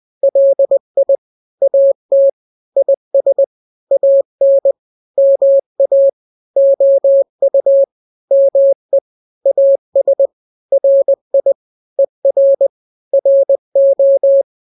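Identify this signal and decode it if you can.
Morse code, heard as a keyed tone, about 20 wpm, 550 Hz. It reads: LI AT IS AN MA OU ME AS RI ER RO
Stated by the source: Morse drill of two-letter n-grams at 20/18 WPM (Farnsworth), from KK7RR.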